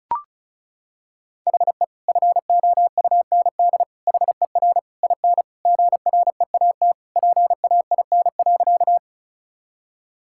Morse code, heard as a keyed tone, about 36 wpm, 700 Hz. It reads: HE FOUND HER IN GREAT PAIN.
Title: Sentences from Top 1600 Words 35wpm